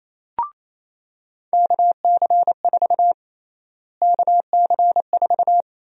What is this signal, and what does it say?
Morse code, heard as a keyed tone, about 28 wpm, 700 Hz. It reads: KC4 KC4
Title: USA Call Sign Prefixes - Repeated 2x - 28wpm